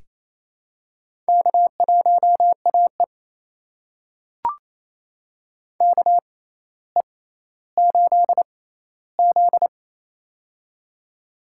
Morse code, heard as a keyed tone, about 28 wpm, spaced as for 14 wpm, 700 Hz. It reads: K1AE KE8Z
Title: USA Call Signs - 4 Characters - 6x Character Spacing - 28wpm